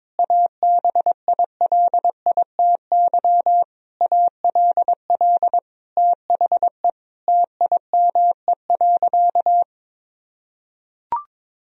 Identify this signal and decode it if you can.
Morse code, heard as a keyed tone, about 22 wpm, 700 Hz. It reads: ABILITY ALL THE TIME.
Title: Sentences from Top 1000 Words - Review Part 3 - 22wpm